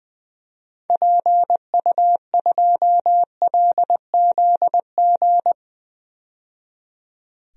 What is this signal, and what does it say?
Morse code, heard as a keyed tone, about 21 wpm, 700 Hz. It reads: PU2LZG
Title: DXCC-100 Call Signs - 6 Characters - 20wpm